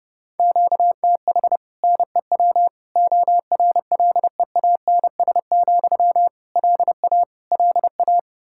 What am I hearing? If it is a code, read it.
Morse code, heard as a keyed tone, about 32 wpm, 700 Hz. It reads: QTH NEW ORLEANS, LA LA
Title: Practice QSO - Exchange 1 - Beginning - 30wpm